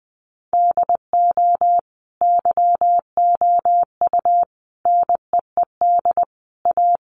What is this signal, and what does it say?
Morse code, heard as a keyed tone, about 20 wpm, 700 Hz. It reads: DO YOU NEED A
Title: Sentences from Top 1500 Words 20wpm